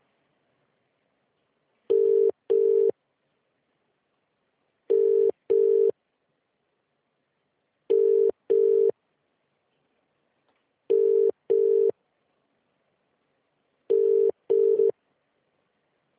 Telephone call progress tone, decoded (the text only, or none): ringback tone